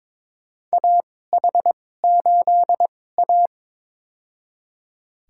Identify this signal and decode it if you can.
Morse code, heard as a keyed tone, about 21 wpm, spaced as for 16 wpm, 700 Hz. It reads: AH8A
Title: USA Call Signs - 2x Character Spacing - 22wpm